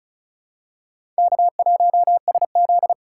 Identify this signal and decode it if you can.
Morse code, heard as a keyed tone, about 36 wpm, 690 Hz. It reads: K1SZ
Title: USA Call Signs - 4 Characters - 2x Character Spacing - 35wpm